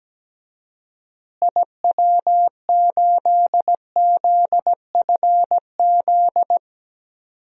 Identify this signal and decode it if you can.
Morse code, heard as a keyed tone, about 17 wpm, 700 Hz. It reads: IW8ZFZ
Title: DXCC-100 Call Signs - 6 Characters - Repeated 3x - 17wpm